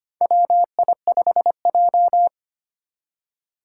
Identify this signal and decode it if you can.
Morse code, heard as a keyed tone, about 25 wpm, 700 Hz. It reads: WI5J
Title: Four Challenging Letter-Number Combinations 25wpm